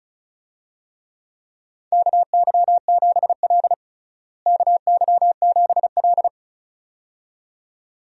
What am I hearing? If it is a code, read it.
Morse code, heard as a keyed tone, about 36 wpm, 700 Hz. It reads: KY7L KY7L
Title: USA Call Signs - 4 Characters - Repeated 2x - 35wpm